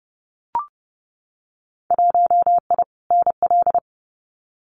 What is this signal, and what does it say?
Morse code, heard as a keyed tone, about 30 wpm, 700 Hz. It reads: 1I NL
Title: Winter Field Day Exchanges 30wpm